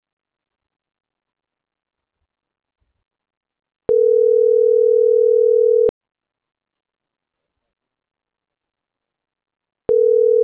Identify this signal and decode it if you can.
Ringback tone (North America), 440 + 480 Hz.